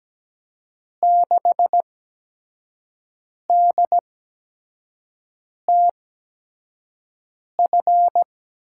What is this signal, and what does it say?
Morse code, heard as a keyed tone, about 17 wpm, spaced as for 5 wpm, 700 Hz. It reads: 6DTF
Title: USA Call Signs - 5 Characters - 8x Character Spacing - 17wpm